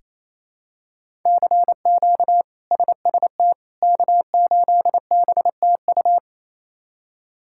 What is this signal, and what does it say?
Morse code, heard as a keyed tone, about 28 wpm, 700 Hz. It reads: CQ SST K8BTU
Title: K1USN SST QSO Practice - 28wpm